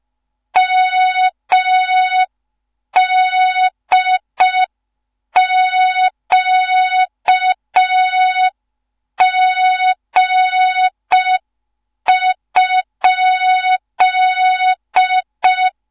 Morse code, heard as a keyed tone, about 5 wpm, 3700 Hz. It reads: MDQG?
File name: Morze-HA2MN / hang / 15 GYAKORLAT